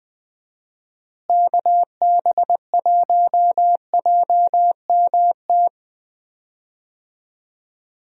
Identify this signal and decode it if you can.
Morse code, heard as a keyed tone, about 20 wpm, 700 Hz. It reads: KB1JMT
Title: USA Call Signs - 6 Characters - Repeated 3x - 20wpm